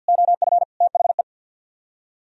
Morse code, heard as a keyed tone, about 64 wpm, spaced as for 56 wpm, 700 Hz. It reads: OF THE
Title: Speed-Racing - Sets of 5 Words 50wpm